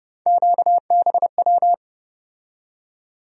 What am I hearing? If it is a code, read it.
Morse code, heard as a keyed tone, about 30 wpm, 700 Hz. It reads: QBW
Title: USA Call Sign Postfixes - 3 Letters 30wpm